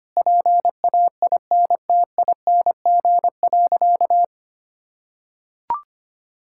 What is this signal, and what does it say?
Morse code, heard as a keyed tone, about 25 wpm, 700 Hz. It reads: PAINTING.
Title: Sentences from Top 1000 Words - Review Part 6 - 25wpm